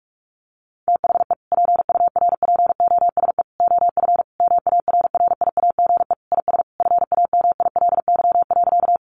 Morse code, heard as a keyed tone, about 44 wpm, 700 Hz. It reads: THE PURPOSE OF MARRIAGE IS FAMILY.
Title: Sentences from Top 800 Words 45wpm